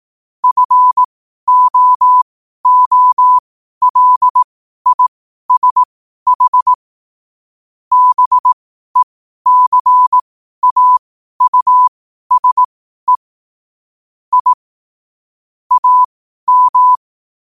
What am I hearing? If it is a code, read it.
Morse code, heard as a keyed tone, about 18 wpm, spaced as for 12 wpm, 1000 Hz. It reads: FOOLISH BECAUSE I AM
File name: qotdfiles / 12wpm